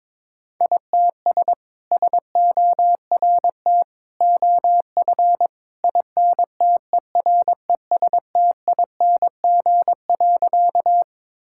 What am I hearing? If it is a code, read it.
Morse code, heard as a keyed tone, about 22 wpm, 700 Hz. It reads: ITS SORT OF INTERESTING.
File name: practice / 22wpm / Sentences with 2-Word Phrases - 22wpm